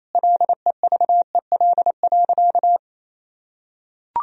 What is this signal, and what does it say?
Morse code, heard as a keyed tone, about 28 wpm, 700 Hz. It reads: LEVEL.
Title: Sentences from Top 1000 Words - Review Part 4 - 28wpm